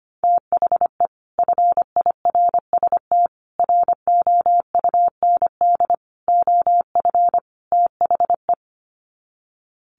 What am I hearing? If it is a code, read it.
Morse code, heard as a keyed tone, about 25 wpm, 700 Hz. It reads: THE FIRST ROUND OF THE
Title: Sets of 5 Words Encore 25wpm